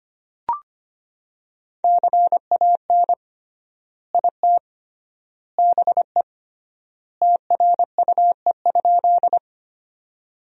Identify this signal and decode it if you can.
Morse code, heard as a keyed tone, about 25 wpm, 700 Hz. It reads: CAN IT BE TRUE?
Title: 3x Word-Spacing - Sentences from Top 400 Words - 25wpm